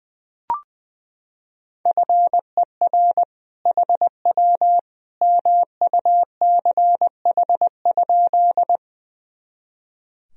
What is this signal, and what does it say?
Morse code, heard as a keyed tone, about 20 wpm, 700 Hz. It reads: FER HW MUCH?
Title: Sentences from Top 200 Words and Abbreviations 20wpm